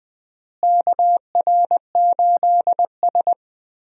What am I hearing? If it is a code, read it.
Morse code, heard as a keyed tone, about 20 wpm, 700 Hz. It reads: KR8S
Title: USA Call Signs - 4 Characters - Repeated 3x - 20wpm